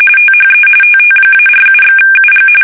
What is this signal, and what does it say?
RTTY, about 45.45 baud. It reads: THE TT-47/UG
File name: fsk850 / teletype-corp